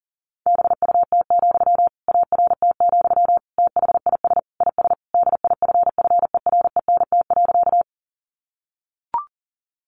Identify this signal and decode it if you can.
Morse code, heard as a keyed tone, about 40 wpm, 700 Hz. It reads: BUT, ART, THIS IS DIFFERENT.
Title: Sentences from Top 600 Words 40wpm